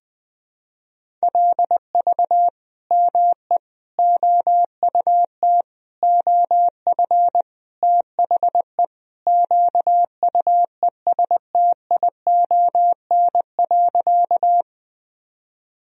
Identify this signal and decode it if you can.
Morse code, heard as a keyed tone, about 21 wpm, 700 Hz. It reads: LV ME OUT OF THE QUESTION.
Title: Sentences from Top 400 Words and Abbreviations 20wpm